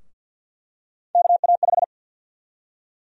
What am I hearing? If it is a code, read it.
Morse code, heard as a keyed tone, about 49 wpm, 700 Hz. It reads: KA5
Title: USA Call Sign Prefixes - 50wpm